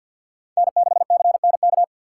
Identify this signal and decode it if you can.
Morse code, heard as a keyed tone, about 63 wpm, 690 Hz. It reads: N6CNK